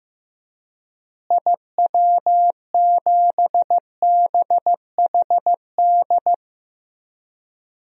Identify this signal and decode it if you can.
Morse code, heard as a keyed tone, about 15 wpm, 700 Hz. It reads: IW7BHD
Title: DXCC-100 Call Signs - 6 Characters - Repeated 3x - 15wpm